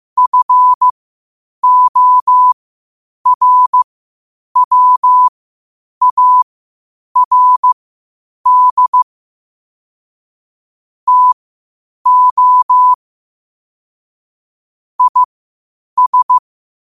Morse code, heard as a keyed tone, about 15 wpm, spaced as for 8 wpm, 1000 Hz. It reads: FORWARD TO IS
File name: qotdfiles / 8wpm